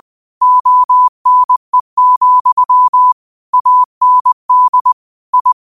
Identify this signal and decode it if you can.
Morse code, heard as a keyed tone, about 20 wpm, 1000 Hz. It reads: ONE, AND I